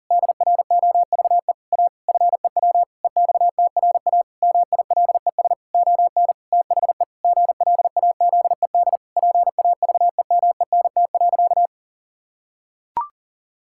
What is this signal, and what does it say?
Morse code, heard as a keyed tone, about 40 wpm, 700 Hz. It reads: DROVE A FEW EXTRA MILES ON THE GLAZED PAVEMENT.
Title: Pangram Challenge v2 - 40wpm